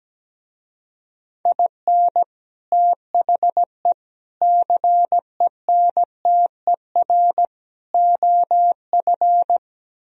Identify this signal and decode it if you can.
Morse code, heard as a keyed tone, about 17 wpm, 700 Hz. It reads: IN THE CENTER OF
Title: Speed-Racing - Sets of 4 Words 17wpm